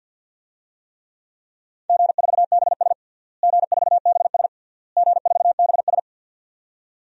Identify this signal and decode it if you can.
Morse code, heard as a keyed tone, about 52 wpm, 700 Hz. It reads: G4BS G4BS G4BS
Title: DXCC-100 Call Signs - 4 Characters - Repeated 3x - 50wpm